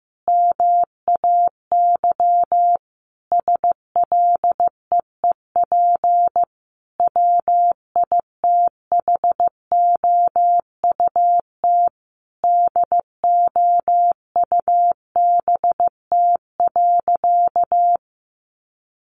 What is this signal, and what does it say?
Morse code, heard as a keyed tone, about 15 wpm, 700 Hz. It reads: MAY SLEEP WITHOUT DOUBT.
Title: Sentences from Top 1500 Words 15wpm